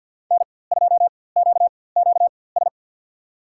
Morse code, heard as a keyed tone, about 50 wpm, spaced as for 23 wpm, 700 Hz. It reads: N2QQS
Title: USA Call Signs - 5 Characters - 4x Character Spacing - 50wpm